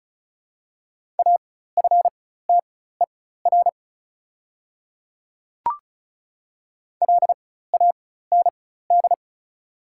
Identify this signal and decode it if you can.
Morse code, heard as a keyed tone, about 36 wpm, spaced as for 16 wpm, 690 Hz. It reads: AFTER LAND
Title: 4x Character Spacing - Top 300 Words - 35wpm